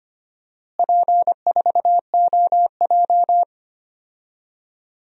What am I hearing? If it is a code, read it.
Morse code, heard as a keyed tone, about 25 wpm, 700 Hz. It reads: P4OJ
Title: DXCC-100 Call Signs - 4 Characters - Repeated 3x - 25wpm